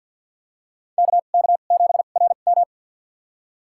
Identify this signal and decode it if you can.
Morse code, heard as a keyed tone, about 51 wpm, spaced as for 36 wpm, 700 Hz. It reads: KK7RM
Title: USA Call Signs - 5 Characters - 2x Character Spacing - 50wpm